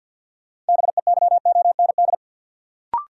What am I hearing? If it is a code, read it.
Morse code, heard as a keyed tone, about 52 wpm, 700 Hz. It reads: BEYOND
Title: Sentences from Top 1100 Words - New Word Drill Alt - 50wpm